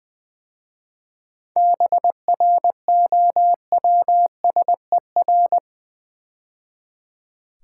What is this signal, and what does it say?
Morse code, heard as a keyed tone, about 20 wpm, 700 Hz. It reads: BROWSER
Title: 7 Letter Words - First 3 Letters Given - 20wpm